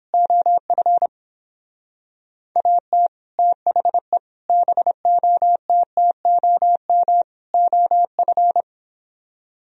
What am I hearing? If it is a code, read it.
Morse code, heard as a keyed tone, about 28 wpm, 700 Hz. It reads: OF AT THE BOTTOM OF